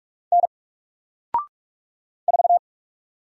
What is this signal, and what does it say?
Morse code, heard as a keyed tone, about 44 wpm, 700 Hz. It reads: N 4
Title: Single Character (Letter:Number:Punctuation:Prosign) 45wpm